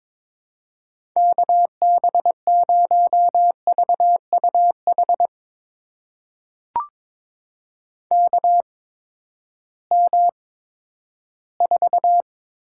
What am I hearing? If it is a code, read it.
Morse code, heard as a keyed tone, about 22 wpm, spaced as for 12 wpm, 700 Hz. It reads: KB0VUH KM4
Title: USA Call Signs - 6 Characters - 8x Character Spacing - 22wpm